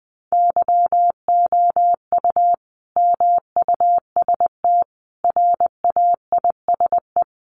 Morse code, heard as a keyed tone, about 20 wpm, 700 Hz. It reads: YOU MUST RAISE
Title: Sentences from Top 1400 Words 20wpm